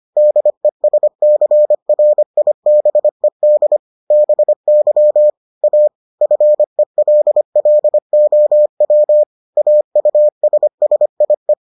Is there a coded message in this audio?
Yes: Morse code from 0.2 to 11.5 s, heard as a keyed tone, about 25 wpm, 600 Hz. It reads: DESCRIBED BY A FELLOW AUSSIE